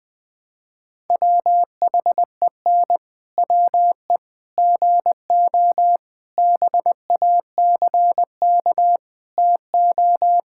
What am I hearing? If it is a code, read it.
Morse code, heard as a keyed tone, about 20 wpm, 700 Hz. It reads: WHEN WE GO BACK TO
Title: Pangram Challenge v2 - 20wpm